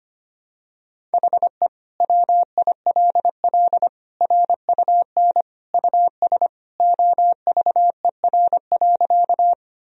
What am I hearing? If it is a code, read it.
Morse code, heard as a keyed tone, about 25 wpm, 700 Hz. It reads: HE WILL RUN US OVER.
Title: Sentences from Top 1000 Words - Review Part 1 - 25wpm